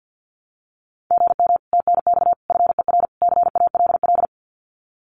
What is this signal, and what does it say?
Morse code, heard as a keyed tone, about 50 wpm, 700 Hz. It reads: GM TNX FER CALL
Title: Common QSO Phrases Encore 50wpm